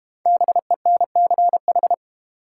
Morse code, heard as a keyed tone, about 32 wpm, 690 Hz. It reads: BENCH